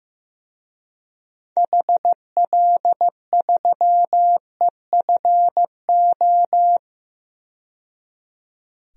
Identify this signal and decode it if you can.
Morse code, heard as a keyed tone, about 15 wpm, 700 Hz. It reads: HL3EFO
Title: DXCC-100 Call Signs - 6 Characters - 15wpm